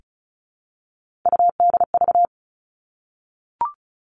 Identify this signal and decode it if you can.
Morse code, heard as a keyed tone, about 34 wpm, 700 Hz. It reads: UDV
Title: All Cut Numbers - 3-Digit 35wpm